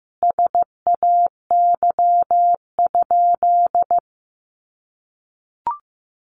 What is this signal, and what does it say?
Morse code, heard as a keyed tone, about 15 wpm, 700 Hz. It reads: SAY?
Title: Sentences from Top 200 Words 15wpm